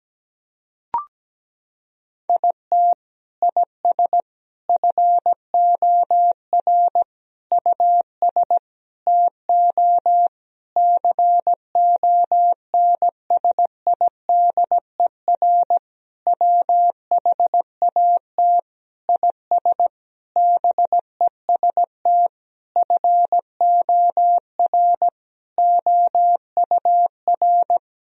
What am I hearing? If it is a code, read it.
Morse code, heard as a keyed tone, about 17 wpm, 700 Hz. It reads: IT IS FOR US TO CONSIDER WHAT IS BEST FOR OUR
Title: Sentences from Top 1000 Words - Review Part 4 - 17wpm